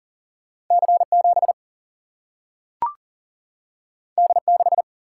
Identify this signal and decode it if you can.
Morse code, heard as a keyed tone, about 39 wpm, 700 Hz. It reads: C 7 D 6
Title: Single-Letter Single-Number 40wpm